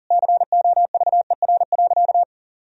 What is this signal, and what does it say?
Morse code, heard as a keyed tone, about 40 wpm, 700 Hz. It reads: COVER.
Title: Sentences from Top 1000 Words - Review Part 1 - 40wpm